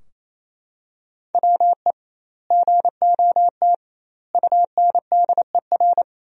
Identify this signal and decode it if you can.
Morse code, heard as a keyed tone, about 28 wpm, 700 Hz. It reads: WE GOT UNDER
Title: 2x Word-Spacing - Sentences from Top 300 Words - 28wpm